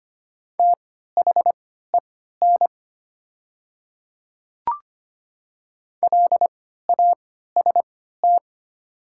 Morse code, heard as a keyed tone, about 25 wpm, spaced as for 14 wpm, 700 Hz. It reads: THEN LAST
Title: 3x Character Spacing - Top 300 Words - 25wpm